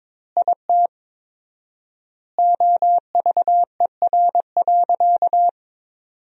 Morse code, heard as a keyed tone, about 22 wpm, 700 Hz. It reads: IT OVER.
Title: 4x Word-Spacing - Sentences from Top 200 Words - 22wpm